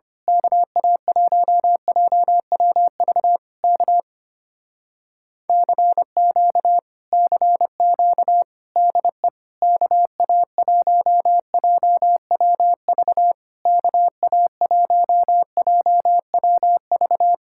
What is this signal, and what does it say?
Morse code, heard as a keyed tone, about 25 wpm, 700 Hz. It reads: KA1JWV K CQ CQ DE KA1JWV KA1JWV
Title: Speed-Racing Common QSO Phrases - 25wpm